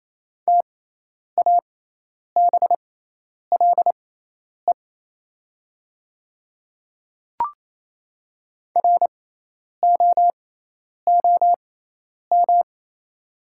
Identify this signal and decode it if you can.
Morse code, heard as a keyed tone, about 28 wpm, spaced as for 10 wpm, 700 Hz. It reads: TABLE ROOM
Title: 6x Character Spacing - Top 500 Words - 28wpm